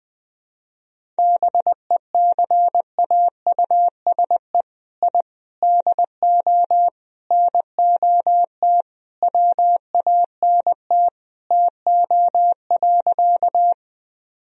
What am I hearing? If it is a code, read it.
Morse code, heard as a keyed tone, about 20 wpm, 700 Hz. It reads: BECAUSE I DO NOT WANT TO.
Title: Sentences from Top 1000 Words - Review Part 3 - 20wpm